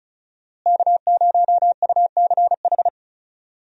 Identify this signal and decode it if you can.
Morse code, heard as a keyed tone, about 36 wpm, 700 Hz. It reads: K0UCH
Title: USA Call Signs - 5 Characters - Repeated 3x - 35wpm